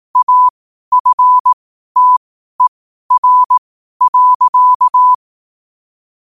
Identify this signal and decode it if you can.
Morse code, heard as a keyed tone, about 18 wpm, spaced as for 12 wpm, 1000 Hz. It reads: AFTER.
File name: qotdfiles / 12wpm